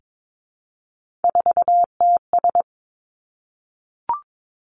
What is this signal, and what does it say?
Morse code, heard as a keyed tone, about 22 wpm, 700 Hz. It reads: 4TS
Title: Three Challenging Letter-Number Combinations 22wpm